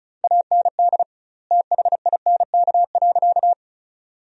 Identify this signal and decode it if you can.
Morse code, heard as a keyed tone, about 36 wpm, 700 Hz. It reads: AND THINK.
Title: 2x Word-Spacing - Sentences from Top 500 Words - 35wpm